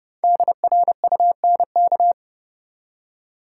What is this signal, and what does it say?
Morse code, heard as a keyed tone, about 31 wpm, 700 Hz. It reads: DRUNK
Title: Five Letter Words - 30wpm